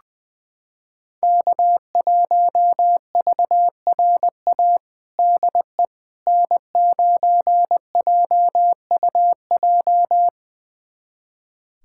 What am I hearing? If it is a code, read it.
Morse code, heard as a keyed tone, about 20 wpm, 700 Hz. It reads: K1VRA DE N9JUJ
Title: USA Callsign DE Callsign - 5 Characters - 20wpm